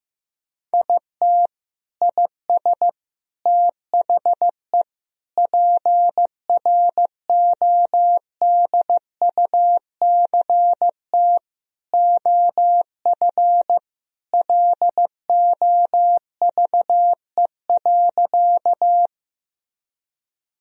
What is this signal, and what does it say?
Morse code, heard as a keyed tone, about 15 wpm, 700 Hz. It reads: IT IS THE PRODUCT OF LOVE.
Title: Sentences from Top 1000 Words - Review Part 2 - 15wpm